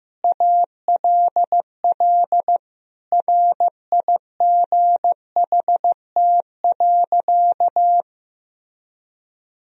Morse code, heard as a keyed tone, about 15 wpm, 700 Hz. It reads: ALL RIGHT.
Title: Sentences from Top 1000 Words - Review Part 4 - 15wpm